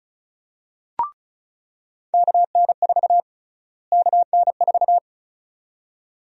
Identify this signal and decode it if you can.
Morse code, heard as a keyed tone, about 34 wpm, 700 Hz. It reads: KN4 KN4
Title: USA Call Sign Prefixes - Repeated 2x - 35wpm